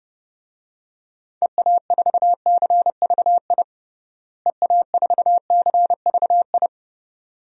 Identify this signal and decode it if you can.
Morse code, heard as a keyed tone, about 30 wpm, 700 Hz. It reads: EA4CVI EA4CVI
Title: DXCC-100 Call Signs - 6 Characters - Repeated 2x - 30wpm